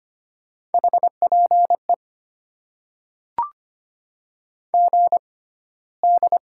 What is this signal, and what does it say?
Morse code, heard as a keyed tone, about 25 wpm, spaced as for 15 wpm, 690 Hz. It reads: HPE GD
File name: Top 90 QSO Elements - 6x Character Spacing - 25wpm